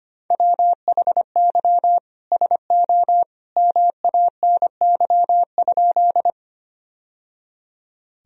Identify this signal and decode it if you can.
Morse code, heard as a keyed tone, about 25 wpm, 700 Hz. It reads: WHY SO MANY?
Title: Sentences from Top 1000 Words - Review Part 1 - 25wpm